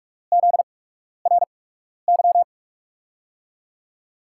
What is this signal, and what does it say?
Morse code, heard as a keyed tone, about 44 wpm, spaced as for 12 wpm, 700 Hz. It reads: ZRY
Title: USA Call Signs - 8x Character Spacing - 45wpm